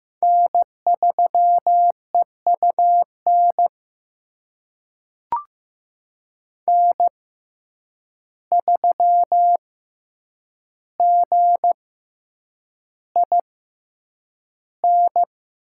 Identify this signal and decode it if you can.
Morse code, heard as a keyed tone, about 15 wpm, spaced as for 8 wpm, 700 Hz. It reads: N3EUN N3GIN
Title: USA Call Signs - 5 Characters - 6x Character Spacing - 15wpm